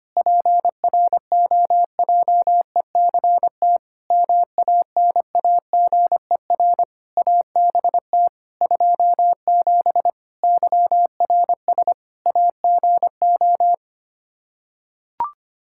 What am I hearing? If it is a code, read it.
Morse code, heard as a keyed tone, about 25 wpm, 700 Hz. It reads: PROJECT MANAGER ABT 27 YRS AGO
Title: Practice QSO - Exchange 3 - More Info - 25wpm